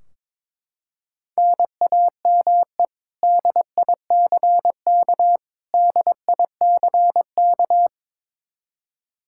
Morse code, heard as a keyed tone, about 22 wpm, 700 Hz. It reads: NAME DICK DICK
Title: Practice QSO - Exchange 1 - Beginning - 22wpm